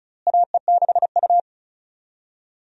Morse code, heard as a keyed tone, about 34 wpm, 700 Hz. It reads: AE6U
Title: USA Call Signs - 4 Characters - Repeated 2x - 35wpm